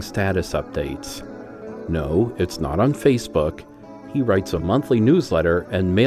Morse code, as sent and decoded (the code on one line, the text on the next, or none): none